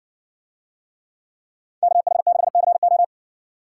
Morse code, heard as a keyed tone, about 66 wpm, 700 Hz. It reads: KH6CO